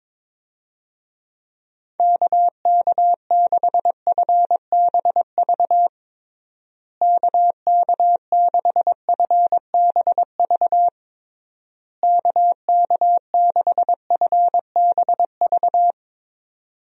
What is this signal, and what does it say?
Morse code, heard as a keyed tone, about 21 wpm, 700 Hz. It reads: KK6FBV KK6FBV KK6FBV
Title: USA Call Signs - 6 Characters - Repeated 3x - 22wpm